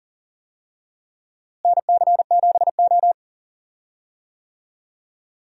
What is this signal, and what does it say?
Morse code, heard as a keyed tone, about 40 wpm, 700 Hz. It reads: NC7O
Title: USA Call Signs - 4 Characters - Repeated 3x - 40wpm